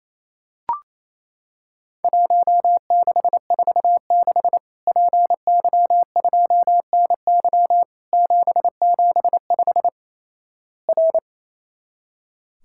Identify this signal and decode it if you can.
Morse code, heard as a keyed tone, about 28 wpm, 700 Hz. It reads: 1646 PY2NY 775 R
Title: Worked All Europe DX Contest - QTC Exchange - 28wpm